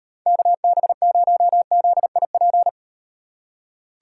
Morse code, heard as a keyed tone, about 38 wpm, 700 Hz. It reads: KB0ZIP